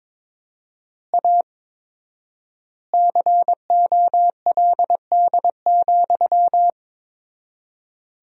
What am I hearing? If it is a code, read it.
Morse code, heard as a keyed tone, about 22 wpm, 700 Hz. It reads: A COLD,
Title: 4x Word-Spacing - Sentences from Top 500 Words - 22wpm